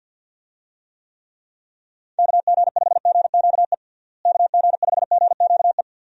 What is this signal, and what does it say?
Morse code, heard as a keyed tone, about 52 wpm, 700 Hz. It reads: KG5GQE KG5GQE
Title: USA Call Signs - 6 Characters - Repeated 3x - 50wpm